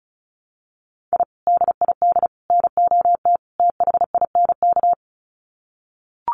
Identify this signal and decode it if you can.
Morse code, heard as a keyed tone, about 34 wpm, 700 Hz. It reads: I DID NOT THINK
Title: Sets of 4 Words 35wpm